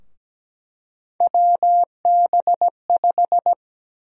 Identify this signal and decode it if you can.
Morse code, heard as a keyed tone, about 17 wpm, 700 Hz. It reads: WB5
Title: USA Call Sign Prefixes - Repeated 2x - 17wpm